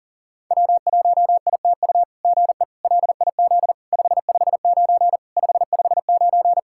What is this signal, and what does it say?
Morse code, heard as a keyed tone, about 40 wpm, 690 Hz. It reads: W1ITU GE LIZ 559 559
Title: SOTA QSO Practice 40wpm